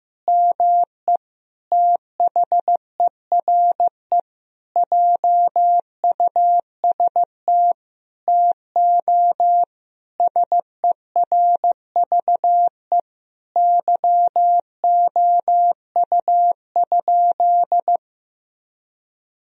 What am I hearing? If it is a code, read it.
Morse code, heard as a keyed tone, about 16 wpm, 700 Hz. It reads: ME THERE JUST TO SERVE YOU?
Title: Sentences from Top 1000 Words - Review Part 2 - 15wpm